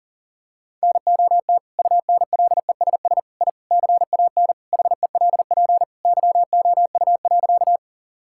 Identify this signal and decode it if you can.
Morse code, heard as a keyed tone, about 40 wpm, 700 Hz. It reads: NOT UNLESS I CAN HELP YOU.
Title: Sentences from Top 1000 Words - Review Part 6 - 40wpm